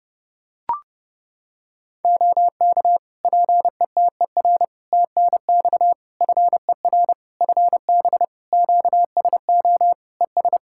Morse code, heard as a keyed tone, about 30 wpm, 700 Hz. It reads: OK PETER TNX FER FB QSO ES
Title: Practice QSO - Exchange 4 - Ending - 30wpm